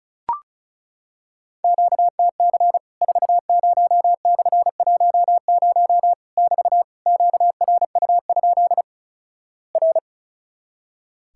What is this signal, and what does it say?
Morse code, heard as a keyed tone, about 36 wpm, 700 Hz. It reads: QTC 40/10 <BT> QRU? R
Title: Worked All Europe DX Contest - QTC Exchange - 35wpm